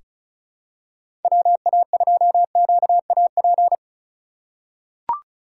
Morse code, heard as a keyed tone, about 36 wpm, 700 Hz. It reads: WA2QAP
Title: USA Call Signs - 8x Character Spacing - 35wpm